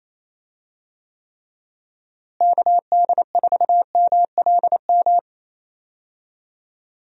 Morse code, heard as a keyed tone, about 28 wpm, 700 Hz. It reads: KD4MLM
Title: USA Call Signs - 6 Characters - Repeated 2x - 28wpm